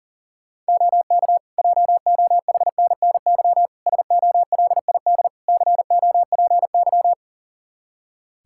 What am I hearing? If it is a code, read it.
Morse code, heard as a keyed tone, about 40 wpm, 700 Hz. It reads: OK JOHNNY SOLID COPY
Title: Practice QSO - Exchange 3 - More Info - 40wpm